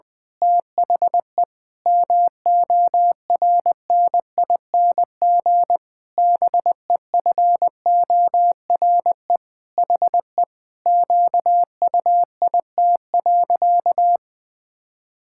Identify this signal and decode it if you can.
Morse code, heard as a keyed tone, about 20 wpm, 700 Hz. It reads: THE MORNING BEFORE HE QUIT.
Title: Pangram Challenge v2 - 20wpm